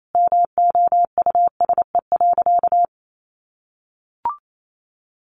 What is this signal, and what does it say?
Morse code, heard as a keyed tone, about 28 wpm, 700 Hz. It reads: MOUSE.
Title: Sentences from Top 700 Words 28wpm